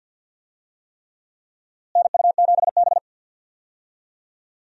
Morse code, heard as a keyed tone, about 47 wpm, 700 Hz. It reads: NU7B
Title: USA Call Signs - 4 Characters - Repeated 3x - 50wpm